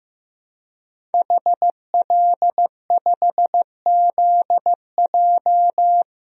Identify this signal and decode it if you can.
Morse code, heard as a keyed tone, about 15 wpm, 700 Hz. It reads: HL5ZJ
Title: DXCC-100 Call Signs - 5 Characters - 15wpm